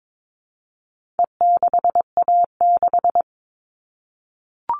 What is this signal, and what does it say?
Morse code, heard as a keyed tone, about 22 wpm, 700 Hz. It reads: E6A6
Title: All Cut Numbers - 4-Digit 22wpm